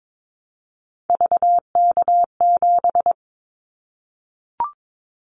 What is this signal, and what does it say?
Morse code, heard as a keyed tone, about 22 wpm, 700 Hz. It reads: VK7
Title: Callsign Prefixes for Top 50 DXCC - 22wpm